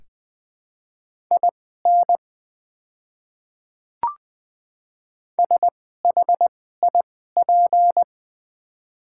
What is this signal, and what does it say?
Morse code, heard as a keyed tone, about 20 wpm, spaced as for 14 wpm, 700 Hz. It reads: IN SHIP
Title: 2x Character Spacing - Top 500 Words - 20wpm